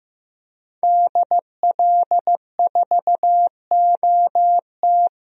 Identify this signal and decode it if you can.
Morse code, heard as a keyed tone, about 15 wpm, 700 Hz. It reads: DL4OT
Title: USA Call Signs - 5 Characters - Repeated 3x - 15wpm